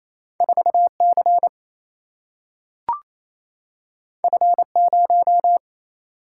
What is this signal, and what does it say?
Morse code, heard as a keyed tone, about 28 wpm, 700 Hz. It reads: 4 C F 0
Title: Single-Letter Single-Number 28wpm